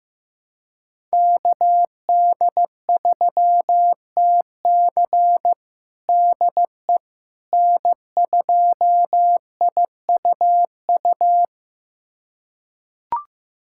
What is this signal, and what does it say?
Morse code, heard as a keyed tone, about 15 wpm, 700 Hz. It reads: KD3TC DE N2IUU
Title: USA Callsign DE Callsign - 5 Characters - 15wpm